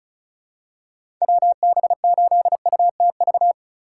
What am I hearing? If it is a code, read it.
Morse code, heard as a keyed tone, about 34 wpm, 700 Hz. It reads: WB8UTV
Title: USA Call Signs - 6 Characters - 2x Character Spacing - 35wpm